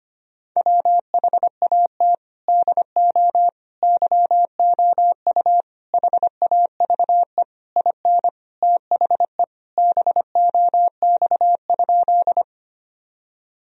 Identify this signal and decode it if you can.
Morse code, heard as a keyed tone, about 25 wpm, 700 Hz. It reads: WHAT DO YOU HAVE IN THE BOX?
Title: Sentences from Top 1000 Words - Review Part 2 - 25wpm